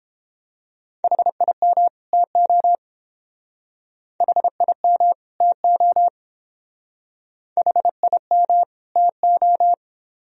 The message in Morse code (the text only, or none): HIM TO HIM TO HIM TO